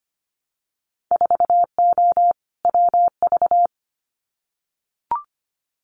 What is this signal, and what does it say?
Morse code, heard as a keyed tone, about 25 wpm, 700 Hz. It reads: 4O WV
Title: Winter Field Day Exchanges Encore 25wpm